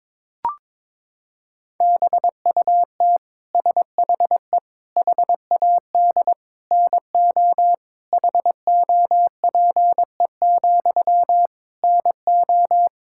Morse code, heard as a keyed tone, about 22 wpm, 700 Hz. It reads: BUT SHE HAD NO HOPE, NO
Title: Sentences from Top 1600 Words 22wpm